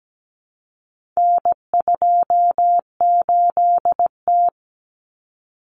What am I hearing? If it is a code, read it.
Morse code, heard as a keyed tone, about 17 wpm, 700 Hz. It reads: N28T
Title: Common Cut Numbers - 4-Digit 17wpm